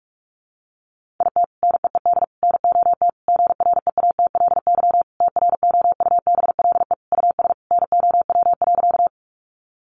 Morse code, heard as a keyed tone, about 44 wpm, 700 Hz. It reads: IT NEED NOT GREATLY TROUBLE US NOW.
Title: Sentences from Top 1400 Words 45wpm